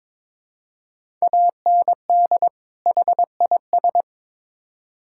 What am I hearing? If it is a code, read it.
Morse code, heard as a keyed tone, about 22 wpm, 700 Hz. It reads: AND HIS
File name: Build A Phrase - 2-Word Phrases - 22wpm